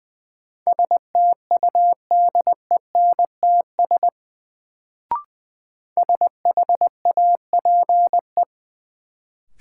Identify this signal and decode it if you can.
Morse code, heard as a keyed tone, about 20 wpm, 700 Hz. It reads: STUDENTS SHAPE
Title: Sets of 5 Words Encore - New Word Drill - 20wpm